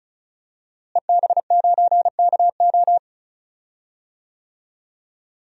Morse code, heard as a keyed tone, about 34 wpm, 700 Hz. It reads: EB9KO
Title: DXCC-100 Call Signs - 5 Characters - Repeated 3x - 35wpm